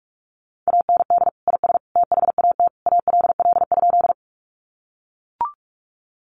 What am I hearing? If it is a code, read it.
Morse code, heard as a keyed tone, about 44 wpm, 700 Hz. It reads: AND IS THAT ALL?
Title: Sentences from Top 100 Words 45wpm